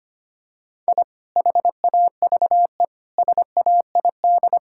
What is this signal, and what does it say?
Morse code, heard as a keyed tone, about 25 wpm, 700 Hz. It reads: I HAVE SAID
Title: Sentences from Top 1000 Words - Review Part 6 - 25wpm